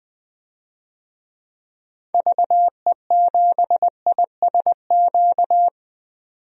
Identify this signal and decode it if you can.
Morse code, heard as a keyed tone, about 20 wpm, 700 Hz. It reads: VE7ISQ
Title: DXCC-100 Call Signs - 6 Characters - 20wpm